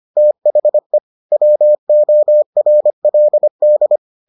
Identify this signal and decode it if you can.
Morse code, heard as a keyed tone, about 25 wpm, 600 Hz. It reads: THE WORLD